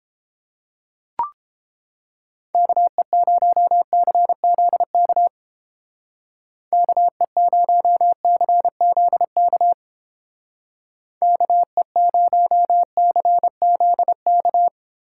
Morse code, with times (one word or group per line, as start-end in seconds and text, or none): 2.5-5.3 KE0CZK
6.7-9.7 KE0CZK
11.2-14.7 KE0CZK